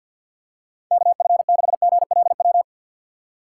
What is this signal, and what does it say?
Morse code, heard as a keyed tone, about 51 wpm, 700 Hz. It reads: KF6GLW
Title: USA Call Signs - 6 Characters - 3x Character Spacing - 50wpm